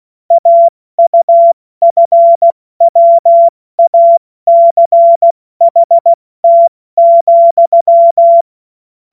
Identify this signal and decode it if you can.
Morse code, heard as a keyed tone, about 16 wpm, 680 Hz. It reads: AUFWACHT,
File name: Morsekurs / Witze / Witz 28. September 2021, 70 F80